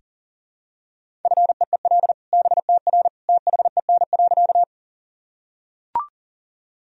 Morse code, heard as a keyed tone, about 40 wpm, 700 Hz. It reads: FEEL BTR THEN.
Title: Sentences from Top 400 Words and Abbreviations 40wpm